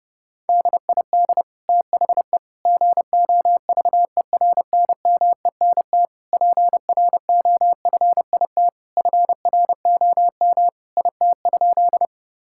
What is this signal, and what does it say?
Morse code, heard as a keyed tone, about 30 wpm, 700 Hz. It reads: DID THE GOVERNMENT PROFIT FROM IT?
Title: Sentences from Top 1000 Words - Review Part 5 - 30wpm